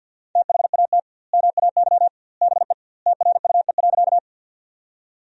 Morse code, heard as a keyed tone, about 50 wpm, 700 Hz. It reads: THAT MAY BE TRUE.
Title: 2x Word-Spacing - Sentences from Top 400 Words - 50wpm